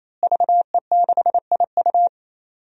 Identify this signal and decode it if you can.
Morse code, heard as a keyed tone, about 28 wpm, 700 Hz. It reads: VE6IU
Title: DXCC-100 Call Signs - 5 Characters - Repeated 3x - 28wpm